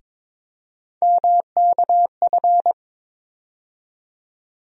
Morse code, heard as a keyed tone, about 21 wpm, 700 Hz. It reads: MKF
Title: USA Call Sign Postfixes - 3 Letters 22wpm